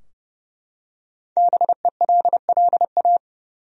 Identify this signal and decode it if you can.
Morse code, heard as a keyed tone, about 30 wpm, 700 Hz. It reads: BELLA
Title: USA Common Names - Max 5 Letters - 30wpm